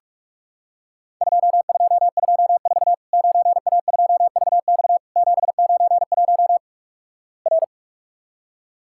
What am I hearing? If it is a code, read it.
Morse code, heard as a keyed tone, about 45 wpm, 700 Hz. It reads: 2224 9A2VX 791 R
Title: Worked All Europe DX Contest - QTC Exchange - 45wpm